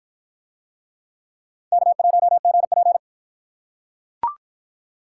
Morse code, heard as a keyed tone, about 45 wpm, 700 Hz. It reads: K1GP